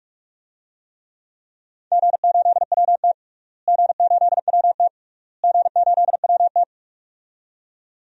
Morse code, heard as a keyed tone, about 42 wpm, 700 Hz. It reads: G8WT G8WT G8WT
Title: DXCC-100 Call Signs - 4 Characters - Repeated 3x - 45wpm